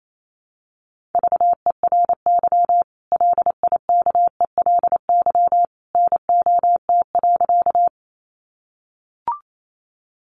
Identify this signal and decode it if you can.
Morse code, heard as a keyed tone, about 28 wpm, 700 Hz. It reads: VERY LIKELY NOT.
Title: Sentences from Top 600 Words 28wpm